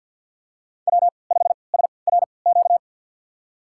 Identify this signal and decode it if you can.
Morse code, heard as a keyed tone, about 48 wpm, spaced as for 28 wpm, 700 Hz. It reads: W5SRQ
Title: USA Call Signs - 5 Characters - 3x Character Spacing - 50wpm